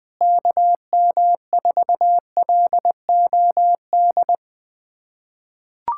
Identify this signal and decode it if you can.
Morse code, heard as a keyed tone, about 21 wpm, 700 Hz. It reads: KM4LOD
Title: USA Callsign DE Callsign - 6 Characters - 20wpm